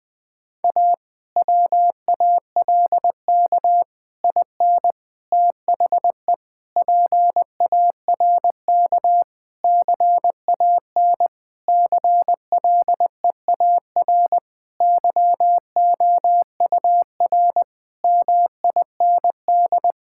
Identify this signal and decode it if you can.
Morse code, heard as a keyed tone, about 20 wpm, 700 Hz. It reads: A WALK IN THE PARK CAN CLEAR YOUR MIND
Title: Difficult Sentences with Hints - 20wpm